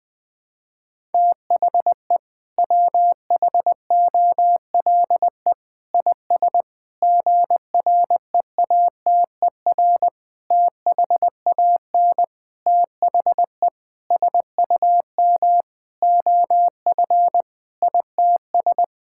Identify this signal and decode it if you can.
Morse code, heard as a keyed tone, about 20 wpm, 700 Hz. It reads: THE WHOLE IS GREATER THAN THE SUM OF ITS
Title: Sentences with 6-Word Phrases - Spoken Beginning - 20wpm